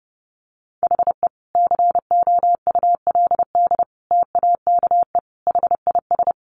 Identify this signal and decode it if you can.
Morse code, heard as a keyed tone, about 30 wpm, 700 Hz. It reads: HE COULD TAKE HIS
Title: Sentences from Top 900 Words 30wpm